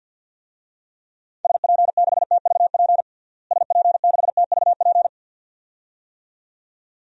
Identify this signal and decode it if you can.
Morse code, heard as a keyed tone, about 52 wpm, 700 Hz. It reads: SP6TVP SP6TVP
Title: DXCC-100 Call Signs - 6 Characters - Repeated 2x - 50wpm